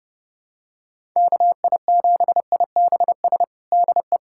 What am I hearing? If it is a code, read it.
Morse code, heard as a keyed tone, about 30 wpm, 700 Hz. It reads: KI7IBS DE
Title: USA Callsign DE Callsign 30wpm